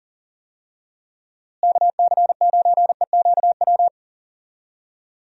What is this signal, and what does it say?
Morse code, heard as a keyed tone, about 37 wpm, 700 Hz. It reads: KC9EQW
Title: USA Call Signs - 6 Characters - Repeated 2x - 40wpm